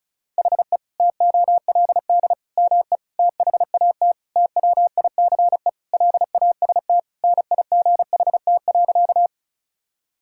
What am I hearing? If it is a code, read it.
Morse code, heard as a keyed tone, about 36 wpm, 700 Hz. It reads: HE TOLD ME THAT TWICE LAST NIGHT.
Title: Sentences from Top 1600 Words 35wpm